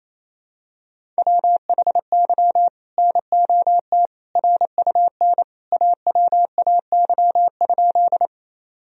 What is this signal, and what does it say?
Morse code, heard as a keyed tone, about 28 wpm, 700 Hz. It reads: WHY NOT RUN AWAY?
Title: Sentences from Top 1000 Words - Review Part 1 - 28wpm